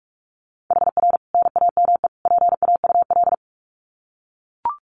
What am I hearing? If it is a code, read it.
Morse code, heard as a keyed tone, about 44 wpm, 700 Hz. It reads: HR NAME PAUL
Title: Common QSO Phrases 45wpm